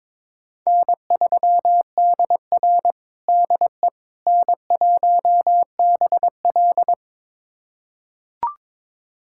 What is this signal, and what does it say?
Morse code, heard as a keyed tone, about 22 wpm, 700 Hz. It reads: N3DR DE N1BL
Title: USA Callsign DE Callsign - 4 Characters - 22wpm